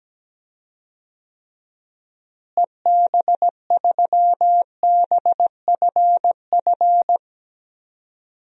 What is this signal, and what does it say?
Morse code, heard as a keyed tone, about 17 wpm, 700 Hz. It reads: EB3BFF
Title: DXCC-100 Call Signs - 6 Characters - Repeated 3x - 17wpm